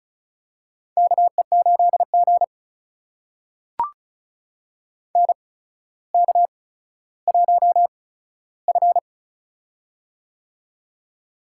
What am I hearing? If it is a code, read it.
Morse code, heard as a keyed tone, about 36 wpm, spaced as for 15 wpm, 700 Hz. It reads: KE8G NK1F
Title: USA Call Signs - 4 Characters - 8x Character Spacing - 35wpm